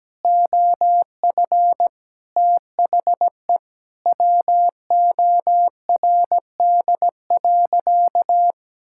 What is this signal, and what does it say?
Morse code, heard as a keyed tone, about 17 wpm, 700 Hz. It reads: OF THE WORD.
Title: Sentences from Top 1000 Words - Review Part 6 - 17wpm